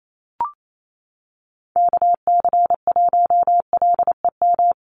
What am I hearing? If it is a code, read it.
Morse code, heard as a keyed tone, about 28 wpm, 700 Hz. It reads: KC1LEM
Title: USA Call Signs Repeated 3x 28wpm